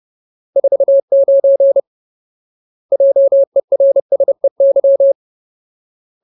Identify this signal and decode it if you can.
Morse code, heard as a keyed tone, about 30 wpm, 550 Hz. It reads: 49 JERSEY
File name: Ultimate Word List / 30WPM Ultimate